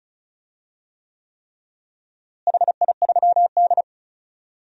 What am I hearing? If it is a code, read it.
Morse code, heard as a keyed tone, about 34 wpm, 700 Hz. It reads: HI3D
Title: DXCC-100 Call Signs - 4 Characters - Repeated 3x - 35wpm